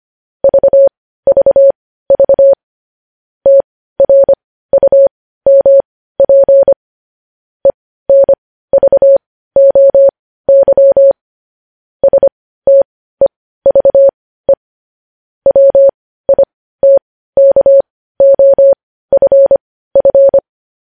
Morse code, heard as a keyed tone, about 25 wpm, spaced as for 15 wpm, 550 Hz. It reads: VVV TRUMP ENVOY STEVE WITKOFF
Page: News Headlines in Morse Code at 15 WPM